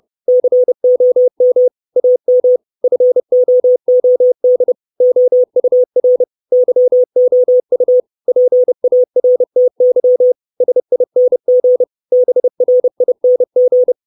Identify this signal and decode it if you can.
Morse code, heard as a keyed tone, about 30 wpm, 500 Hz. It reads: COM AM FOOD OUR YOU PARTY SING BRING
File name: 632 Words 30WPM